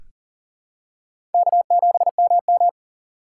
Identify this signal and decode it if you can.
Morse code, heard as a keyed tone, about 37 wpm, 700 Hz. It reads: K7MM
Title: USA Call Signs - 4 Characters - 4x Character Spacing - 40wpm